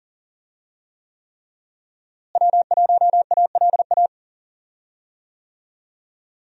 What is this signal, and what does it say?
Morse code, heard as a keyed tone, about 40 wpm, 700 Hz. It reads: W1ALA
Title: USA Call Signs - 5 Characters - Repeated 3x - 40wpm